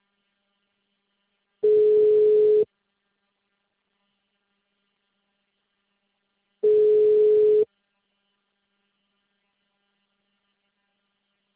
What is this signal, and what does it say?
Ringback tone (Europe, 425 Hz).